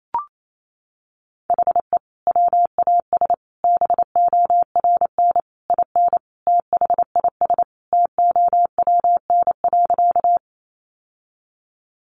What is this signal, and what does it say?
Morse code, heard as a keyed tone, about 28 wpm, 700 Hz. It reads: HE WAS BORN IN THIS TOWN.
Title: Sentences from Top 1000 Words 28wpm